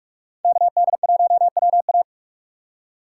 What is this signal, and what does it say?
Morse code, heard as a keyed tone, about 44 wpm, 700 Hz. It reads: KD1WA
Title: USA Call Signs - 5 Characters - 4x Character Spacing - 45wpm